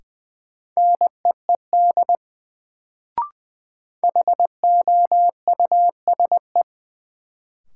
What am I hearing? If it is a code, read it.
Morse code, heard as a keyed tone, about 20 wpm, 700 Hz. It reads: NEED HOUSE
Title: Sets of 2 Words - New Word Drill - 20wpm